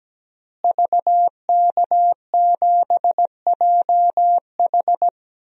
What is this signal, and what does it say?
Morse code, heard as a keyed tone, about 17 wpm, 700 Hz. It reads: VK7JH
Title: DXCC-100 Call Signs - 5 Characters - Repeated 2x - 17wpm